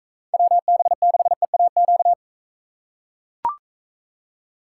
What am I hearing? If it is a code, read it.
Morse code, heard as a keyed tone, about 43 wpm, 700 Hz. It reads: WB6EAQ